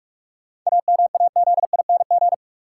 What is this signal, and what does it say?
Morse code, heard as a keyed tone, about 46 wpm, 700 Hz. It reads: AMAZING
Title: Sentences from Top 1500 Words - New Word Drill - 45wpm